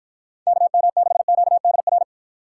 Morse code, heard as a keyed tone, about 53 wpm, 700 Hz. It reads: KM6QDR